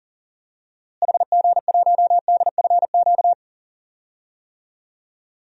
Morse code, heard as a keyed tone, about 40 wpm, 700 Hz. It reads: HG1DFQ
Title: DXCC-100 Call Signs - 6 Characters - Repeated 3x - 40wpm